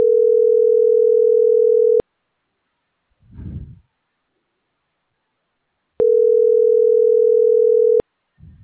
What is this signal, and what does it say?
Ringback tone (North America), 440 + 480 Hz.